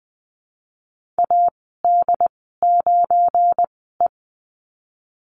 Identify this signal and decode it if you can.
Morse code, heard as a keyed tone, about 20 wpm, spaced as for 14 wpm, 700 Hz. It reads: AD9E
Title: 179 Lesson 29g - Slash - Call Signs 20wpm